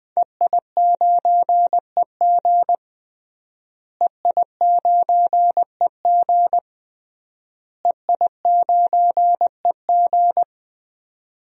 Morse code, heard as a keyed tone, about 20 wpm, 700 Hz. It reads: EI9EG EI9EG EI9EG
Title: DXCC-100 Call Signs - 5 Characters - Repeated 3x - 20wpm